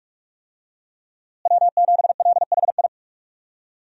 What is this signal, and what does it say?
Morse code, heard as a keyed tone, about 41 wpm, 700 Hz. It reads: W7LHI
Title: USA Call Signs - 5 Characters - Repeated 3x - 45wpm